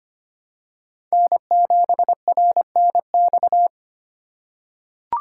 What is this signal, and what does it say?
Morse code, heard as a keyed tone, about 25 wpm, 700 Hz. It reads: N7RNX
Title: USA Call Signs - 8x Character Spacing - 25wpm